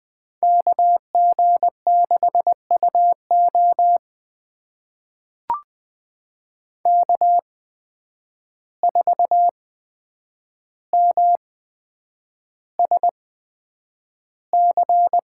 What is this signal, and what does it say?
Morse code, heard as a keyed tone, about 20 wpm, spaced as for 9 wpm, 700 Hz. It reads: KG6UO K4MSC
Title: USA Call Signs - 5 Characters - 8x Character Spacing - 20wpm